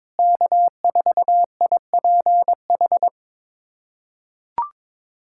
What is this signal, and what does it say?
Morse code, heard as a keyed tone, about 22 wpm, 700 Hz. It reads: K4IPH
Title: USA Callsign DE Callsign 22wpm